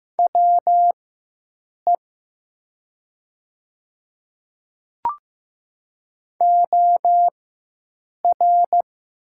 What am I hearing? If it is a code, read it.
Morse code, heard as a keyed tone, about 15 wpm, spaced as for 7 wpm, 700 Hz. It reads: WE OR
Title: 4x Character Spacing - Top 100 Words - 15wpm